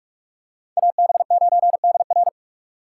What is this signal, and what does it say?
Morse code, heard as a keyed tone, about 45 wpm, 700 Hz. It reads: AB9DR